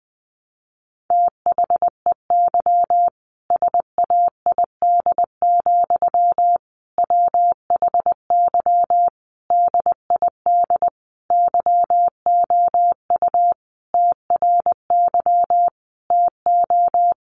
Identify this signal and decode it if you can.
Morse code, heard as a keyed tone, about 20 wpm, 700 Hz. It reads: THEY SAID, WHY DID YOU TRY TO
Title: Sentences from Top 1200 Words 20wpm